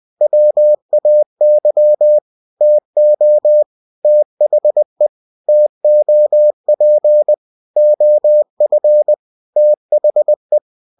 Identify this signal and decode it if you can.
Morse code, heard as a keyed tone, about 20 wpm, 600 Hz. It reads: WAY TO THE TOP OF THE
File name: cw-books / winnie-the-pooh / winnie-20wpm-page-0076